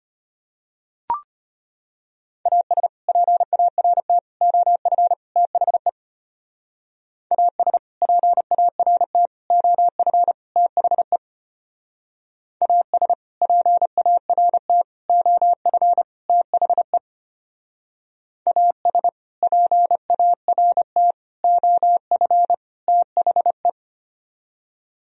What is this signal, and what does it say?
Morse code, heard as a keyed tone, about 31 wpm, 700 Hz. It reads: AS PART OF THE AS PART OF THE AS PART OF THE AS PART OF THE
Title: Speed-Racing - Sets of 4 Words 25wpm